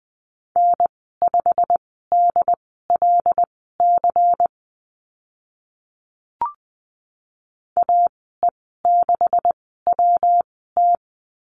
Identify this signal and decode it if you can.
Morse code, heard as a keyed tone, about 20 wpm, spaced as for 14 wpm, 700 Hz. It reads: N5DLC AE6WT
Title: 140 Lesson 23g - Letter f - Call Signs 20wpm